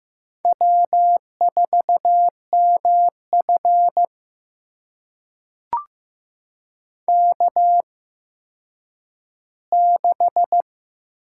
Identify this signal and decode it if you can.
Morse code, heard as a keyed tone, about 15 wpm, spaced as for 9 wpm, 700 Hz. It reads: W4MF K6
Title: USA Call Signs - 4 Characters - 8x Character Spacing - 15wpm